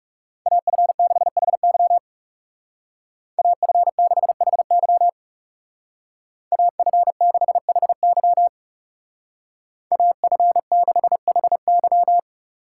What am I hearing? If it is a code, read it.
Morse code, heard as a keyed tone, about 35 wpm, 700 Hz. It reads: AF6HY AF6HY AF6HY AF6HY
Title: Speed Racing - USA 2x2 Call Signs 30wpm